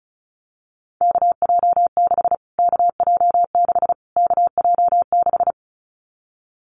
Morse code, heard as a keyed tone, about 34 wpm, 700 Hz. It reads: KJ6 KJ6 KJ6
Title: Callsign Prefixes for Top 25 DXCC - Repeated 3x - 35wpm